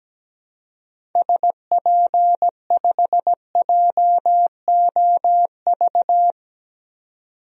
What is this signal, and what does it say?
Morse code, heard as a keyed tone, about 17 wpm, 700 Hz. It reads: SP5JOV
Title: DXCC-100 Call Signs - 6 Characters - 17wpm